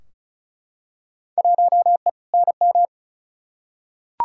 Morse code, heard as a keyed tone, about 36 wpm, 700 Hz. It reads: 1E NM
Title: Field Day Exchanges - 35wpm